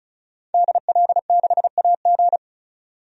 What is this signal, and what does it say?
Morse code, heard as a keyed tone, about 36 wpm, 700 Hz. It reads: DL6AG
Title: DXCC-100 Call Signs - 5 Characters - 35wpm